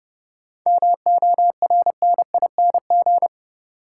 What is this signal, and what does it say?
Morse code, heard as a keyed tone, about 30 wpm, 700 Hz. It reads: MORNING
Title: Sets of 3 Words - New Word Drill - 30wpm